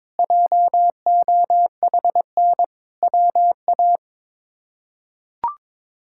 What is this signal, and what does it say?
Morse code, heard as a keyed tone, about 22 wpm, 700 Hz. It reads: JOHN WA
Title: K1USN SST QSO Practice - 22wpm